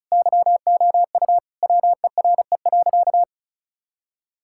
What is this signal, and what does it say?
Morse code, heard as a keyed tone, about 34 wpm, 700 Hz. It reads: YOU WERE.
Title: Sentences from Top 1000 Words - Review Part 6 - 35wpm